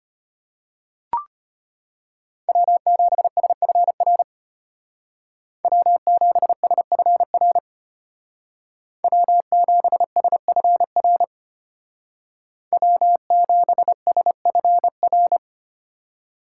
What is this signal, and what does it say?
Morse code, heard as a keyed tone, about 31 wpm, 700 Hz. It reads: W7SFR W7SFR W7SFR W7SFR
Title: Speed Racing - USA 1x3 Call Signs 25wpm